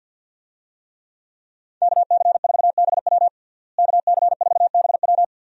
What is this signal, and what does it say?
Morse code, heard as a keyed tone, about 52 wpm, 700 Hz. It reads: KC4BW KC4BW
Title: USA Call Signs - 5 Characters - Repeated 3x - 50wpm